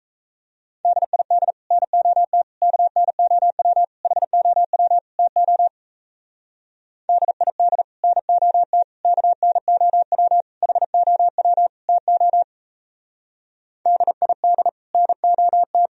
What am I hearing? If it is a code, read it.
Morse code, heard as a keyed tone, about 35 wpm, 700 Hz. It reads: DID NOT KNOW HOW TO DID NOT KNOW HOW TO DID NOT